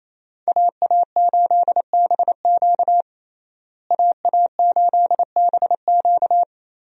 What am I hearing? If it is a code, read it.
Morse code, heard as a keyed tone, about 28 wpm, 700 Hz. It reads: AA8BQ AA8BQ
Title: USA Call Signs - 5 Characters - Repeated 3x - 28wpm